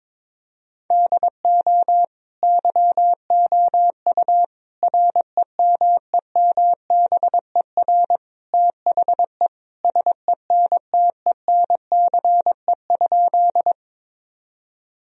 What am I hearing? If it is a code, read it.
Morse code, heard as a keyed tone, about 22 wpm, 700 Hz. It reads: DO YOU REMEMBER THE SENTENCE?
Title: Sentences from Top 1000 Words - Review Part 1 - 22wpm